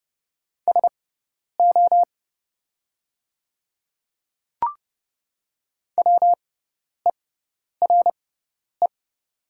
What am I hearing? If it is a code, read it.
Morse code, heard as a keyed tone, about 30 wpm, spaced as for 10 wpm, 700 Hz. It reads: SO WERE
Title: 6x Character Spacing - Top 100 Words - 30wpm